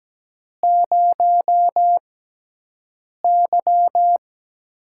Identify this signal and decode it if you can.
Morse code, heard as a keyed tone, about 17 wpm, spaced as for 6 wpm, 700 Hz. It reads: 0Y
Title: USA Call Signs - 4 Characters - 6x Character Spacing - 17wpm